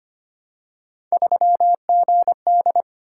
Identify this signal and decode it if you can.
Morse code, heard as a keyed tone, about 25 wpm, 700 Hz. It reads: 3GD
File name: Three Letter-Number Combinations - 25wpm